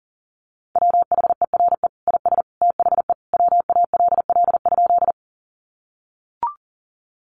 Morse code, heard as a keyed tone, about 40 wpm, 700 Hz. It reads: WHERE IS THE WALL?
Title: Sentences from Top 1100 Words 40wpm